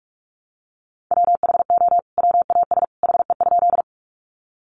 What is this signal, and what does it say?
Morse code, heard as a keyed tone, about 44 wpm, 700 Hz. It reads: WHO WAS HE?
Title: Sentences from Top 100 Words 45wpm